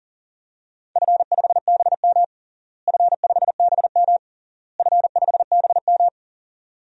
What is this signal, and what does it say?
Morse code, heard as a keyed tone, about 40 wpm, 700 Hz. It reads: F5BM F5BM F5BM
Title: DXCC-100 Call Signs - 4 Characters - Repeated 3x - 40wpm